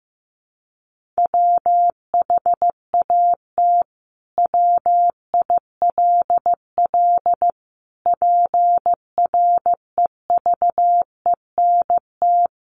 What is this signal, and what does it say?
Morse code, heard as a keyed tone, about 15 wpm, 700 Hz. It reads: WHAT WILL PREVENT
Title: Sentences from Top 1100 Words 15wpm